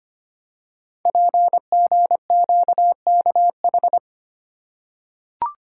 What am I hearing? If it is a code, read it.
Morse code, heard as a keyed tone, about 25 wpm, 700 Hz. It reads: PGQKH
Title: Code Groups - 5 Letters - 25wpm